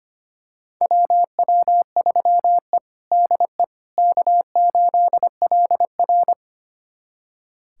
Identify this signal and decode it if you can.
Morse code, heard as a keyed tone, about 25 wpm, 700 Hz. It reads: WW3E DE K8LR
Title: USA Callsign DE Callsign - 4 Characters - 25wpm